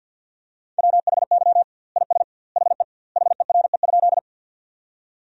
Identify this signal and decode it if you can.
Morse code, heard as a keyed tone, about 46 wpm, 700 Hz. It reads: WHY IS HE HERE?
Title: 2x Word-Spacing - Sentences from Top 200 Words - 50wpm